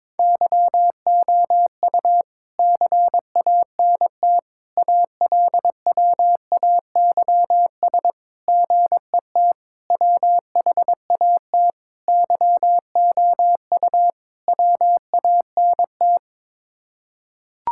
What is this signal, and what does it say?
Morse code, heard as a keyed tone, about 21 wpm, 700 Hz. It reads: YOU CANT ALWAYS GET WHAT YOU WANT
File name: practice / 22wpm / Sets of 7 Words - 22wpm